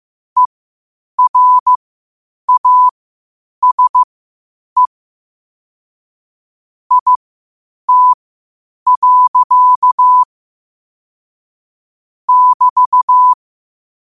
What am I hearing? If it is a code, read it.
Morse code, heard as a keyed tone, about 15 wpm, spaced as for 8 wpm, 1000 Hz. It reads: ERASE IT. <BT>
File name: qotdfiles / 8wpm